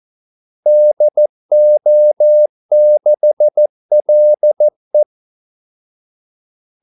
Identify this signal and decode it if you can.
Morse code, heard as a keyed tone, about 14 wpm, 600 Hz. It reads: DO6LE
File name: Callsigns-14wpm